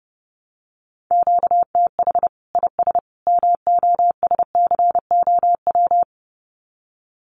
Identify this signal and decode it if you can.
Morse code, heard as a keyed tone, about 30 wpm, 700 Hz. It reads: QTH IS MOSCOW
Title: Common QSO Phrases 30wpm